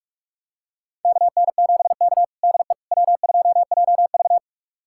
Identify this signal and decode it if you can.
Morse code, heard as a keyed tone, about 46 wpm, 700 Hz. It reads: KN7K DE W2JV
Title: USA Callsign DE Callsign - 4 Characters - 45wpm